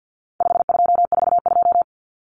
Morse code, heard as a keyed tone, about 49 wpm, 700 Hz. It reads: 5242
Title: Numbers - 4-Digit 50wpm